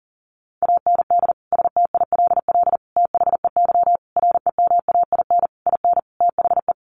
Morse code, heard as a keyed tone, about 40 wpm, 700 Hz. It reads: AND STILL THEY REMAIN IN THE
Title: Sentences from Top 1200 Words 40wpm